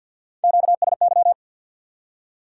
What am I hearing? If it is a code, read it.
Morse code, heard as a keyed tone, about 54 wpm, 700 Hz. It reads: QSY
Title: Top 90 QSO Elements 50wpm